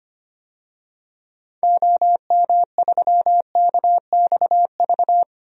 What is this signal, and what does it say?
Morse code, heard as a keyed tone, about 25 wpm, 700 Hz. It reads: OM3KXV
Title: DXCC-100 Call Signs - 6 Characters - 25wpm